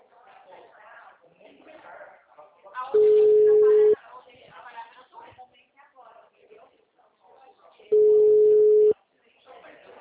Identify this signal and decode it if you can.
Ringback tone (Europe, 425 Hz), with speech.